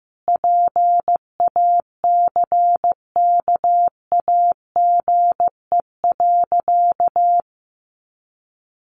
Morse code, heard as a keyed tone, about 15 wpm, 700 Hz. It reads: PACKAGE.
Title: Additional Easy Phrases 15wpm